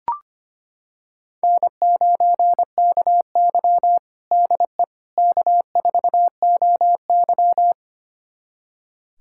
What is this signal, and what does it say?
Morse code, heard as a keyed tone, about 25 wpm, 700 Hz. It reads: N9KY DE K4OY
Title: USA Callsign DE Callsign - 4 Characters - 25wpm